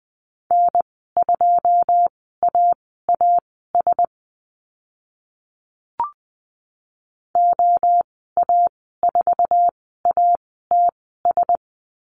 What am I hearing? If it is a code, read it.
Morse code, heard as a keyed tone, about 20 wpm, spaced as for 14 wpm, 700 Hz. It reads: N2AAS OA4ATS
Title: 088 Lesson 15g - Number 5 - Call Signs 20wpm